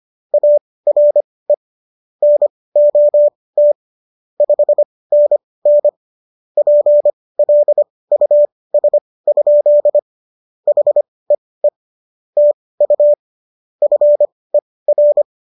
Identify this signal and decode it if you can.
Morse code, heard as a keyed tone, about 25 wpm, spaced as for 18 wpm, 590 Hz. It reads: ARE NOT 5NN PLUS? HEE TU FER